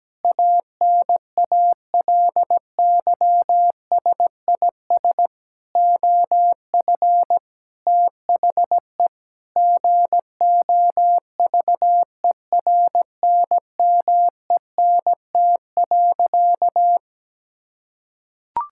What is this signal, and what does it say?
Morse code, heard as a keyed tone, about 17 wpm, 700 Hz. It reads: ANALYSIS OF THE GOVERNMENT.
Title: Sentences from Top 1000 Words - Review Part 5 - 17wpm